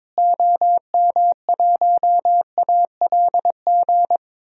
Morse code, heard as a keyed tone, about 22 wpm, 700 Hz. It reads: OM1ALG
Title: DXCC-100 Call Signs - 6 Characters - Repeated 2x - 22wpm